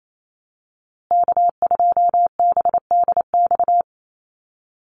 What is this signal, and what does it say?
Morse code, heard as a keyed tone, about 28 wpm, 700 Hz. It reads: K2BDX
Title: USA Call Signs Repeated 2x 28wpm